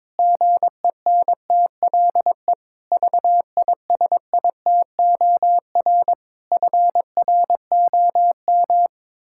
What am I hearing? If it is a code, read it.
Morse code, heard as a keyed tone, about 22 wpm, 700 Hz. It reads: GENTLE VISITOR FROM
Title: Pangram Challenge v2 - 22wpm